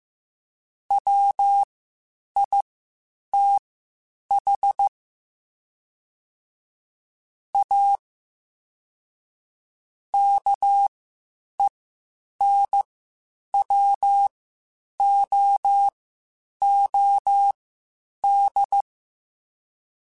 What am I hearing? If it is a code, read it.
Morse code, heard as a keyed tone, about 15 wpm, spaced as for 8 wpm, 800 Hz. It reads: WITH A KENWOOD
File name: QSOs / QSOs-mp3s / WD0BGP 1@10